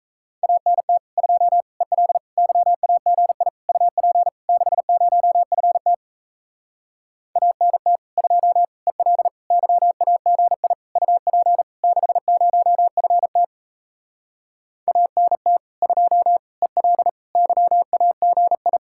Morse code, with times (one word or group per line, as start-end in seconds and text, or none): 0.4-1.0 ANT
1.2-1.6 2
1.8-2.2 EL
2.4-3.5 YAGI
3.7-4.3 UP
4.5-5.9 60FT
7.4-8.0 ANT
8.2-8.6 2
8.9-9.3 EL
9.5-10.7 YAGI
11.0-11.6 UP
11.8-13.4 60FT
14.9-15.6 ANT
15.8-16.4 2
16.6-17.1 EL
17.4-18.8 YAGI